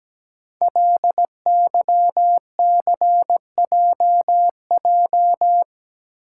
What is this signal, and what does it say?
Morse code, heard as a keyed tone, about 17 wpm, 700 Hz. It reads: LYCJJ
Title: Code Groups - 5 Letter:Numbers - 17wpm